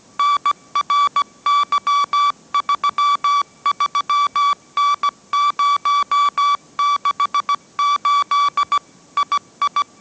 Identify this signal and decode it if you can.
Morse code, heard as a keyed tone, about 17 wpm, 1150 Hz. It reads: NRY33N068 II
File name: mrx / mrx080 18wpm